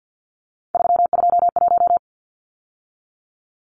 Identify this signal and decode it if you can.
Morse code, heard as a keyed tone, about 50 wpm, 700 Hz. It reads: 321
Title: Numbers - 3-Digit 50wpm